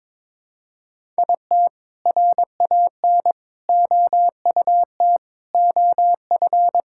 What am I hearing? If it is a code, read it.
Morse code, heard as a keyed tone, about 22 wpm, 690 Hz. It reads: IT RAN OUT OF
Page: morse-code-ninja-practice-22wpm